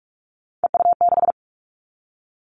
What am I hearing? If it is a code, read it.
Morse code, heard as a keyed tone, about 46 wpm, 700 Hz. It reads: EU6
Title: Callsign Prefixes for Top 50 DXCC - 45wpm